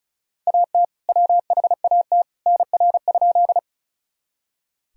Morse code, heard as a keyed tone, about 36 wpm, 700 Hz. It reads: AT WHAT NR?
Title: Sentences from Top 100 Words and Abbreviations 35wpm